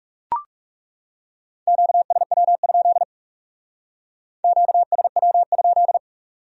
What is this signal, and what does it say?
Morse code, heard as a keyed tone, about 36 wpm, 700 Hz. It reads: QSW? QSW?
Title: Speed-Racing Q-Codes - 30wpm